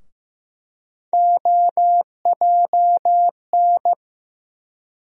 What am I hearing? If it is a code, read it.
Morse code, heard as a keyed tone, about 15 wpm, 700 Hz. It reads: OJN
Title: USA Call Sign Postfixes - 3 Letters 15wpm